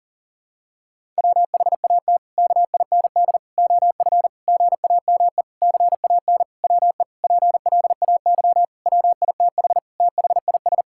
Morse code, heard as a keyed tone, about 37 wpm, 700 Hz. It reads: WHAT KIND OF GAME CAN WE PLAY WITH THIS
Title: Sentences from Top 1000 Words - Review Part 6 - 40wpm